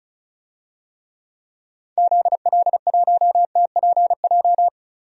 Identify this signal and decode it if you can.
Morse code, heard as a keyed tone, about 36 wpm, 700 Hz. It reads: ZL1TPJ
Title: DXCC-100 Call Signs - 6 Characters - Repeated 3x - 35wpm